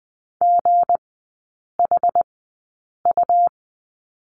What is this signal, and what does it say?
Morse code, heard as a keyed tone, about 20 wpm, spaced as for 8 wpm, 690 Hz. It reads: GHU
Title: 159 Lesson 26e - Letter g - Three Characters 20wpm